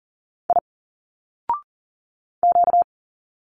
Morse code, heard as a keyed tone, about 40 wpm, 700 Hz. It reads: I Q
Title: Single Letters 40wpm